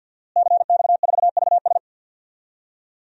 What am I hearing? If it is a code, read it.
Morse code, heard as a keyed tone, about 50 wpm, 700 Hz. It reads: CX4VS
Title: DXCC-100 Call Signs - 5 Characters - Repeated 3x - 50wpm